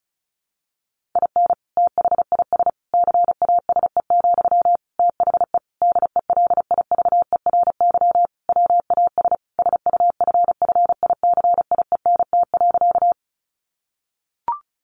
Morse code, heard as a keyed tone, about 34 wpm, 700 Hz. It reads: IN THIS CASE, THE DELIVERY WAS SUFFICIENT.
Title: Sentences from Top 1500 Words 35wpm